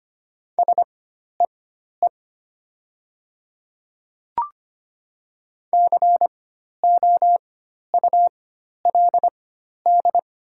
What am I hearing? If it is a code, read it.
Morse code, heard as a keyed tone, about 25 wpm, spaced as for 12 wpm, 700 Hz. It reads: SEE COULD
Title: 4x Character Spacing - Top 100 Words - 25wpm